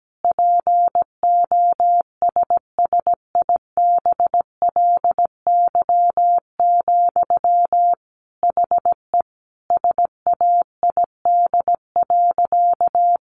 Morse code, heard as a keyed tone, about 17 wpm, 700 Hz. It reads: POSSIBLY, HE SAID.
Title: Sentences from Top 1300 Words 17wpm